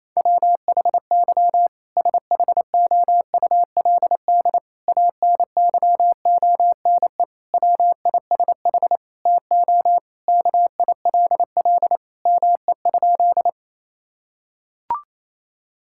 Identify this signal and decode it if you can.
Morse code, heard as a keyed tone, about 28 wpm, 700 Hz. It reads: WHY SHOULD ANYONE WISH TO KILL ME?
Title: Sentences from Top 1600 Words 28wpm